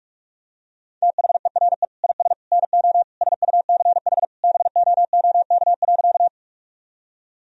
Morse code, heard as a keyed tone, about 46 wpm, 700 Hz. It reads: THERE IS NO SUCH BOOK.
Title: Sentences from Top 1000 Words - Review Part 1 - 45wpm